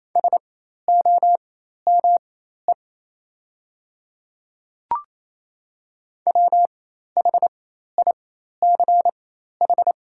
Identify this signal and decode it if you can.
Morse code, heard as a keyed tone, about 27 wpm, spaced as for 13 wpm, 700 Hz. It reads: SOME WHICH
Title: 4x Character Spacing - Top 500 Words - 28wpm